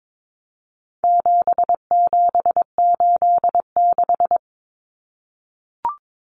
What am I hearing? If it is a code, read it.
Morse code, heard as a keyed tone, about 22 wpm, 700 Hz. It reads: 7786
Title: Numbers - 4-Digit 22wpm